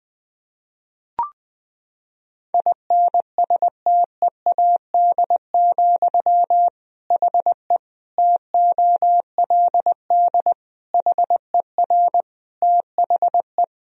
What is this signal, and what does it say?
Morse code, heard as a keyed tone, about 20 wpm, 700 Hz. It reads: INSTEAD, HE TOLD HER THE
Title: Sentences from Top 1000 Words - Review Part 4 - 20wpm